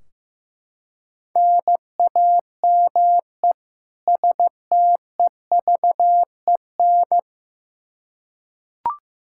Morse code, heard as a keyed tone, about 15 wpm, 700 Hz. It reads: NAME STEVEN
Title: Practice QSO - Exchange 1 - Beginning - 15wpm